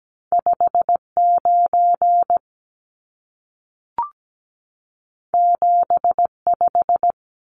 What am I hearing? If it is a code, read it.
Morse code, heard as a keyed tone, about 17 wpm, 700 Hz. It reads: 59 75
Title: Numbers - 2-Digit 17wpm